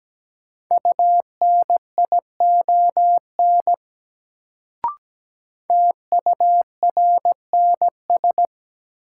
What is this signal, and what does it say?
Morse code, heard as a keyed tone, about 17 wpm, 700 Hz. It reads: UNION TURNS
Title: Sets of 5 Words Encore - New Word Drill - 17wpm